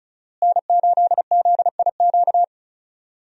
Morse code, heard as a keyed tone, about 36 wpm, 700 Hz. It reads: N8ZIQ
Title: USA Call Signs - 5 Characters - Repeated 2x - 35wpm